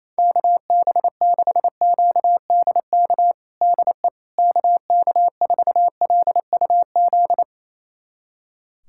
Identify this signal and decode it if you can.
Morse code, heard as a keyed tone, about 28 wpm, 700 Hz. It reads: KB6QDK DE KK4LUZ
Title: USA Callsign DE Callsign 28wpm